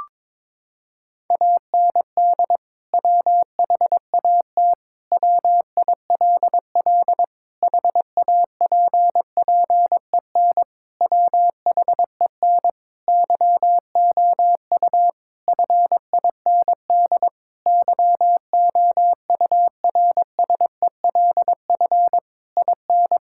Morse code, heard as a keyed tone, about 22 wpm, 700 Hz. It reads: AND WHAT WILL HAPPEN WHEN YOU FIND YOURSELF IN
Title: Sentences from Top 1000 Words - Review Part 5 - 22wpm